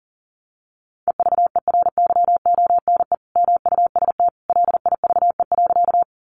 Morse code, heard as a keyed tone, about 40 wpm, 700 Hz. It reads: EVERYONE MUST LIVE.
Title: Sentences from Top 800 Words 40wpm